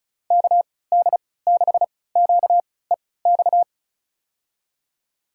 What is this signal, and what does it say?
Morse code, heard as a keyed tone, about 36 wpm, spaced as for 20 wpm, 700 Hz. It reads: KD6QEX
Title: USA Call Signs - 6 Characters - 3x Character Spacing - 35wpm